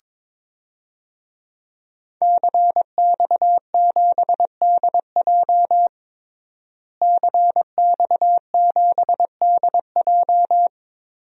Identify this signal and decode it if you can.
Morse code, heard as a keyed tone, about 22 wpm, 700 Hz. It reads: CX7DJ CX7DJ
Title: DXCC-100 Call Signs - 5 Characters - Repeated 3x - 22wpm